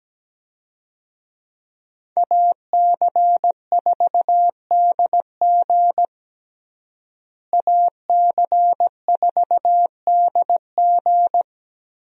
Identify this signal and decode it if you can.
Morse code, heard as a keyed tone, about 17 wpm, 700 Hz. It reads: AC4DG AC4DG
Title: USA Call Signs - 5 Characters - Repeated 3x - 17wpm